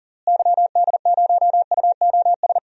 Morse code, heard as a keyed tone, about 40 wpm, 700 Hz. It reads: YD0UOS
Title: DXCC-100 Call Signs - 6 Characters - Repeated 2x - 40wpm